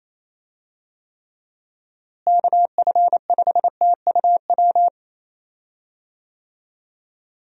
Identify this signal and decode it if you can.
Morse code, heard as a keyed tone, about 28 wpm, 700 Hz. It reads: KF5TUW
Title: USA Call Signs - 6 Characters - Repeated 3x - 28wpm